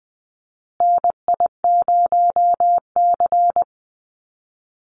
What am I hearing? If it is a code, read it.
Morse code, heard as a keyed tone, about 19 wpm, 700 Hz. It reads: NI0C
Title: Call Signs Repeated 2x - 20wpm